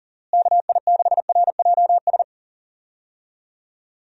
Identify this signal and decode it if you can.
Morse code, heard as a keyed tone, about 40 wpm, 700 Hz. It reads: KI6RJS
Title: USA Call Signs - 6 Characters - Repeated 2x - 40wpm